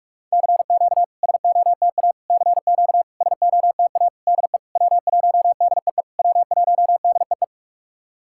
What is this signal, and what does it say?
Morse code, heard as a keyed tone, about 44 wpm, 690 Hz. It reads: CQ SOTA CQ SOTA DE W1DEE W1DEE
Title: SOTA QSO Practice 45wpm